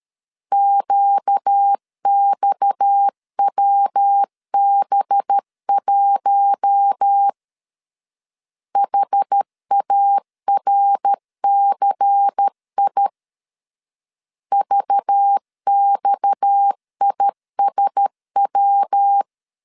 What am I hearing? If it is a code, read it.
Morse code, heard as a keyed tone, about 13 wpm, 800 Hz. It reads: QXWB1 HARCI VXISW
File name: LEZIONI / 6 PKX0 / 6 PKX 12wpm 2024